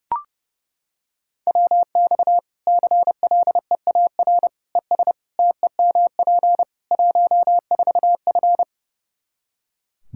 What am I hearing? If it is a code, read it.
Morse code, heard as a keyed tone, about 30 wpm, 700 Hz. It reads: WX CLEAR ES TEMP 14F
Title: K1USN SST QSO Practice 30wpm